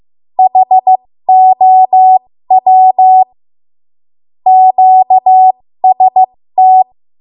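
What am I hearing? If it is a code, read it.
Morse code, heard as a keyed tone, about 15 wpm, spaced as for 13 wpm, 750 Hz. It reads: HOW QST
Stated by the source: 13 wpm Morse code practice archive transmission.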